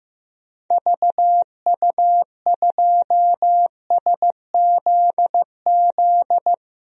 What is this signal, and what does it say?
Morse code, heard as a keyed tone, about 15 wpm, 700 Hz. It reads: VU2SZZ
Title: DXCC-100 Call Signs - 6 Characters - 15wpm